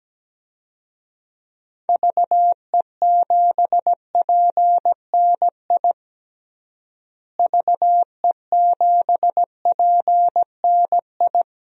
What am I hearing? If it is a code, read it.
Morse code, heard as a keyed tone, about 17 wpm, 700 Hz. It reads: VE7PNI VE7PNI
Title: DXCC-100 Call Signs - 6 Characters - Repeated 3x - 17wpm